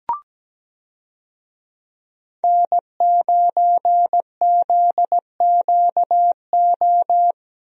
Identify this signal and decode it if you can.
Morse code, heard as a keyed tone, about 17 wpm, 700 Hz. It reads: N9ZQO
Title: USA Call Signs - 5 Characters - Repeated 2x - 17wpm